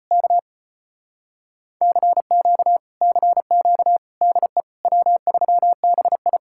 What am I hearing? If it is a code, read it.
Morse code, heard as a keyed tone, about 30 wpm, 700 Hz. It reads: K CQ CQ DE W3BI
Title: Speed-Racing Common QSO Phrases Encore - 25wpm